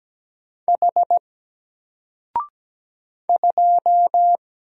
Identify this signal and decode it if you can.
Morse code, heard as a keyed tone, about 17 wpm, 700 Hz. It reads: H 2
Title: Single Character (Letter:Number:Punctuation:Prosign) Alt 17wpm